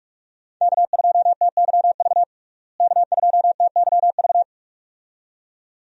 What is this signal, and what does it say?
Morse code, heard as a keyed tone, about 47 wpm, 700 Hz. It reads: K2TYV K2TYV
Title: USA Call Signs - 5 Characters - Repeated 2x - 45wpm